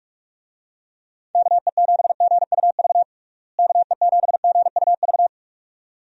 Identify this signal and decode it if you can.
Morse code, heard as a keyed tone, about 46 wpm, 700 Hz. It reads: KE7GUV KE7GUV
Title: USA Call Signs - 6 Characters - Repeated 2x - 45wpm